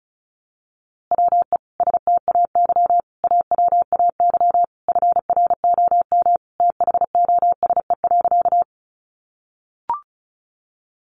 Morse code, heard as a keyed tone, about 34 wpm, 700 Hz. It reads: WE STAY AWAY FROM THOSE.
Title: A Sound Of Thunder 35wpm